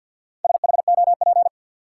Morse code, heard as a keyed tone, about 53 wpm, 700 Hz. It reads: SHOP
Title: Sentences from Top 1600 Words - New Word Drill Alt - 50wpm